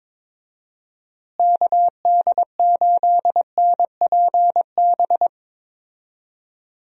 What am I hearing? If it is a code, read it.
Morse code, heard as a keyed tone, about 22 wpm, 700 Hz. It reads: KD8NPB
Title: USA Call Signs - 6 Characters - Repeated 3x - 22wpm